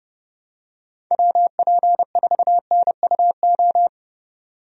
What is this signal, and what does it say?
Morse code, heard as a keyed tone, about 30 wpm, 700 Hz. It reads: WP4NUO
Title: USA Call Signs - 6 Characters - Repeated 3x - 30wpm